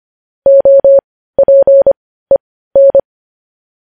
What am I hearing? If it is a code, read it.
Morse code, heard as a keyed tone, about 25 wpm, spaced as for 15 wpm, 550 Hz. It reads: OPEN
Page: news headlines in morse code at 15 wpm